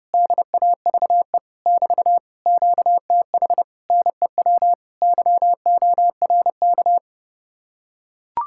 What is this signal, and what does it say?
Morse code, heard as a keyed tone, about 30 wpm, 700 Hz. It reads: DAVE <BT> QTH NEW YORK